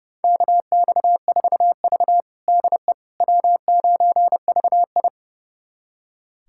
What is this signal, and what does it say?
Morse code, heard as a keyed tone, about 30 wpm, 700 Hz. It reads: KX4V DE W9VI
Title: USA Callsign DE Callsign - 4 Characters - 30wpm